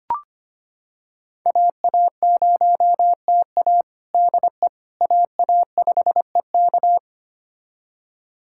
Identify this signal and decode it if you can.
Morse code, heard as a keyed tone, about 25 wpm, 700 Hz. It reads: AA0TA DE AA5EK
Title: USA Callsign DE Callsign - 5 Characters - 25wpm